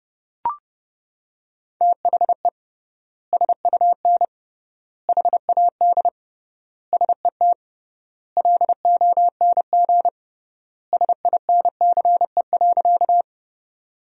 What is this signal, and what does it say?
Morse code, heard as a keyed tone, about 32 wpm, 700 Hz. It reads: THE SUN HAD SET LONG SINCE.
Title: 3x Word-Spacing - Sentences from Top 300 Words - 30wpm